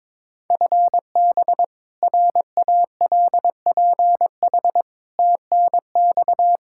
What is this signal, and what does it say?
Morse code, heard as a keyed tone, about 21 wpm, 700 Hz. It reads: FB RALPH TNX
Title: Practice QSO - Exchange 3 - More Info - 22wpm